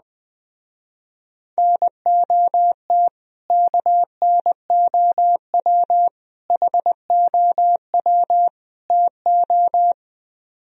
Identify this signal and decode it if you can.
Morse code, heard as a keyed tone, about 20 wpm, 700 Hz. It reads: NOT KNOW HOW TO
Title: Speed-Racing - Sets of 4 Words 17wpm